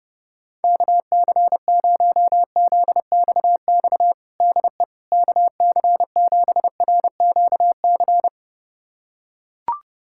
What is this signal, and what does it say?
Morse code, heard as a keyed tone, about 32 wpm, 700 Hz. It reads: KC0ZXX DE KC7RQC
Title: USA Callsign DE Callsign 30wpm